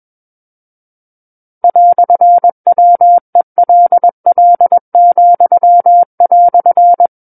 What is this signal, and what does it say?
Morse code, heard as a keyed tone, about 21 wpm, 700 Hz. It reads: "WELL,"
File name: Book One / 12-Chapter 12 - 21 WPM